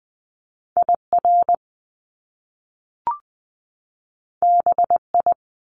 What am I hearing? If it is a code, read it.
Morse code, heard as a keyed tone, about 20 wpm, 700 Hz. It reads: IR BI
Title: Top Two Letter Combinations in Words 20wpm